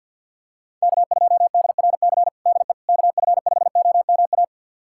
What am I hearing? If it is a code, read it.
Morse code, heard as a keyed tone, about 51 wpm, 690 Hz. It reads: K2DRC DE KF5OMA
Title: USA Callsign DE Callsign 50wpm